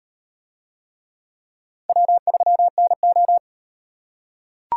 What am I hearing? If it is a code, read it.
Morse code, heard as a keyed tone, about 38 wpm, 700 Hz. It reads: W3NO